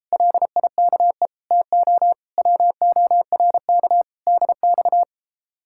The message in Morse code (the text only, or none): LIKE TO WORK DX